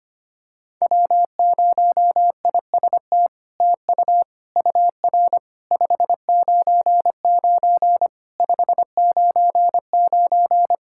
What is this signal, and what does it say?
Morse code, heard as a keyed tone, about 25 wpm, 700 Hz. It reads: W0IST TU UR 599 599
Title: SOTA QSO Practice 25wpm